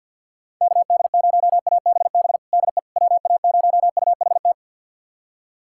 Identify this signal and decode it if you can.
Morse code, heard as a keyed tone, about 46 wpm, 700 Hz. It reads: KD0ABB DE WA0UHT
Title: USA Callsign DE Callsign - 6 Characters - 50wpm